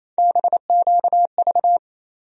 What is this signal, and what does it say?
Morse code, heard as a keyed tone, about 27 wpm, 700 Hz. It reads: BQV
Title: USA Call Sign Postfixes - 3 Letters 28wpm